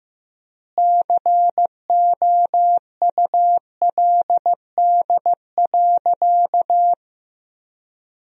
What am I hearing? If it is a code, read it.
Morse code, heard as a keyed tone, about 15 wpm, 700 Hz. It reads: COULD.
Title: 3x Word-Spacing - Sentences from Top 500 Words - 15wpm